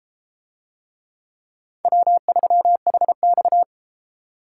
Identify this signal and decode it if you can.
Morse code, heard as a keyed tone, about 31 wpm, 700 Hz. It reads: W3HX